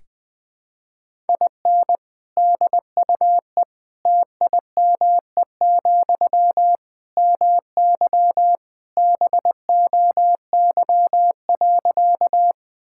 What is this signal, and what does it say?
Morse code, heard as a keyed tone, about 20 wpm, 700 Hz. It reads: IN DUE TIME, MY BOY.
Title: Sentences from Top 1000 Words - Review Part 6 - 20wpm